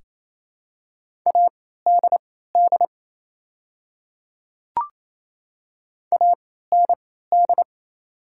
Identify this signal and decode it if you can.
Morse code, heard as a keyed tone, about 27 wpm, spaced as for 16 wpm, 700 Hz. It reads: ADD AND
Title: 3x Character Spacing - Top 200 Words - 28wpm